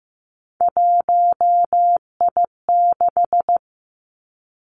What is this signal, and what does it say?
Morse code, heard as a keyed tone, about 15 wpm, 700 Hz. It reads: 1I6
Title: Three Challenging Letter-Number Combinations 15wpm